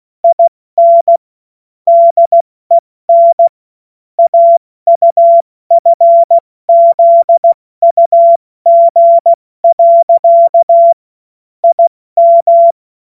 Morse code, heard as a keyed tone, about 16 wpm, 680 Hz. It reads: IN DEN AUFZUG. IM
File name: Morsekurs / Witze / Witz 31. August 2021, 70 F80